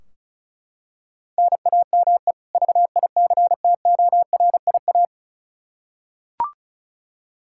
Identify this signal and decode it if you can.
Morse code, heard as a keyed tone, about 34 wpm, 700 Hz. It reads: NAME VICTORIA
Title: Practice QSO - Exchange 1 - Beginning - 35wpm